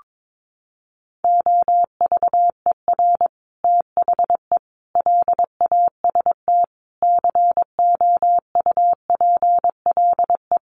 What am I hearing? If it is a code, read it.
Morse code, heard as a keyed tone, about 22 wpm, 700 Hz. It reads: OVER THE LAST COUPLE
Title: Sets of 5 Words Encore 22wpm